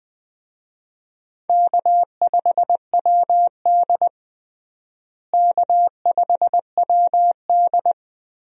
Morse code, heard as a keyed tone, about 20 wpm, 700 Hz. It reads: K5WD K5WD
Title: USA Call Signs - 4 Characters - Repeated 3x - 20wpm